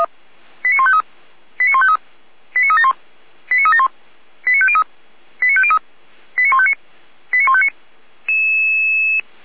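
Five-tone selective call (ZVEI), heard as quick runs of five tones, tones of about 70 ms. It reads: 89153 89153 89371 89371 89503 89503 89159 89159
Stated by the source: ZVEI five-tone call 89160 received by SDR, recorded by feuerwehr-orlamuende (fire brigade Orlamünde).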